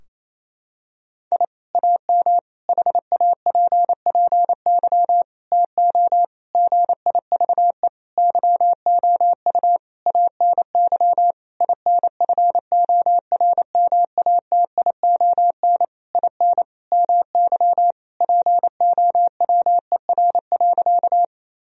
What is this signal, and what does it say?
Morse code, heard as a keyed tone, about 28 wpm, 700 Hz. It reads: I AM HAPPY TO GIVE YOU ANY INFORMATION IN MY POWER.
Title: Sentences from Top 1000 Words - Review Part 3 - 28wpm